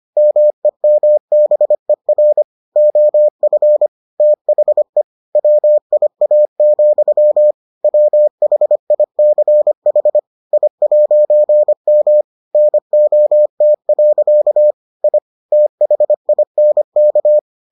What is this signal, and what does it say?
Morse code, heard as a keyed tone, about 25 wpm, 600 Hz. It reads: MEMBER OF THE WIA, WHICH I'M NOT. I THINK